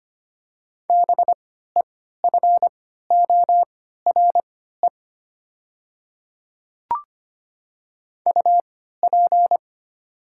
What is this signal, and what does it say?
Morse code, heard as a keyed tone, about 25 wpm, spaced as for 14 wpm, 700 Hz. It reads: BEFORE UP
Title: 3x Character Spacing - Top 500 Words - 25wpm